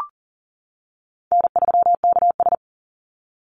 Morse code, heard as a keyed tone, about 40 wpm, 700 Hz. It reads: N3KS
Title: Easy Call Signs - 40wpm